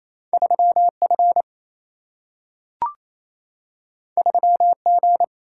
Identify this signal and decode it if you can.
Morse code, heard as a keyed tone, about 28 wpm, 700 Hz. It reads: 3F 3G
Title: Single-Letter Single-Number 28wpm